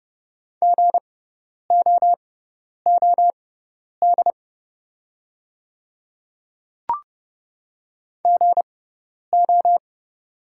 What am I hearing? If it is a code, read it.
Morse code, heard as a keyed tone, about 30 wpm, spaced as for 10 wpm, 690 Hz. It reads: GOOD GO
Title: 6x Character Spacing - Top 200 Words - 30wpm